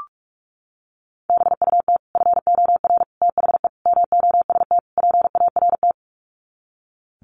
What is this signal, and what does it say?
Morse code, heard as a keyed tone, about 44 wpm, 700 Hz. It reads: BUT FOR THE MOST PART
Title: Sets of 5 Words Encore 45wpm